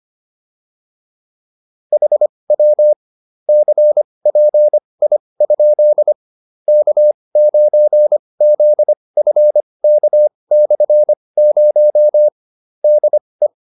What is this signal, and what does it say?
Morse code, heard as a keyed tone, about 25 wpm, spaced as for 20 wpm, 600 Hz. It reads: HW CPI? K9ZFK/0 DE
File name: QSO303a 20